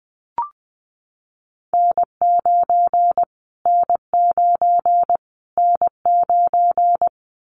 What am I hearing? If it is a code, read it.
Morse code, heard as a keyed tone, about 20 wpm, 700 Hz. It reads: N9 N9 N9
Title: Callsign Prefixes for Top 25 DXCC - Repeated 3x - 20wpm